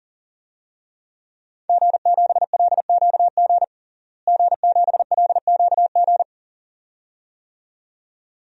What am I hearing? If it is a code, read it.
Morse code, heard as a keyed tone, about 37 wpm, 700 Hz. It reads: G7LQG G7LQG
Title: DXCC-100 Call Signs - 5 Characters - Repeated 2x - 40wpm